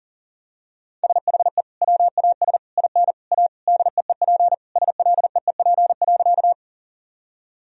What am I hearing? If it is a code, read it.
Morse code, heard as a keyed tone, about 40 wpm, 700 Hz. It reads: SHE WAS IN A DEEP SLEEP.
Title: Sentences from Top 1000 Words - Review Part 3 - 40wpm